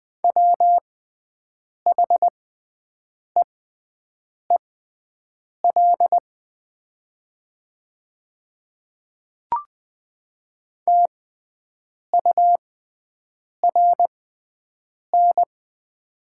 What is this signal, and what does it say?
Morse code, heard as a keyed tone, about 20 wpm, spaced as for 7 wpm, 700 Hz. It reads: WHEEL TURN
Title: 6x Character Spacing - Top 500 Words - 20wpm